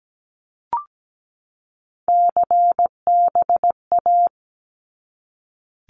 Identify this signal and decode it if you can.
Morse code, heard as a keyed tone, about 17 wpm, 700 Hz. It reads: CBA
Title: Abbreviations 17wpm